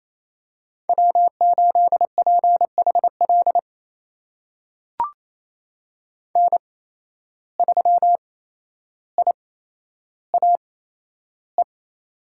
Morse code, heard as a keyed tone, about 28 wpm, spaced as for 12 wpm, 700 Hz. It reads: W8PHL N3IAE
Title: USA Call Signs - 5 Characters - 8x Character Spacing - 28wpm